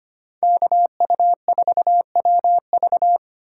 Morse code, heard as a keyed tone, about 25 wpm, 700 Hz. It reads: KU4WV
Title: USA Call Signs - 5 Characters - 8x Character Spacing - 25wpm